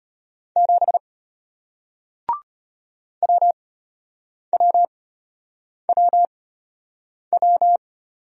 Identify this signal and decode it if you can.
Morse code, heard as a keyed tone, about 33 wpm, spaced as for 8 wpm, 700 Hz. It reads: 7 WWWW